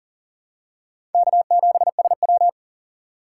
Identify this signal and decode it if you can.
Morse code, heard as a keyed tone, about 40 wpm, 700 Hz. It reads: K7SW
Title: USA Call Signs - 4 Characters - 2x Character Spacing - 40wpm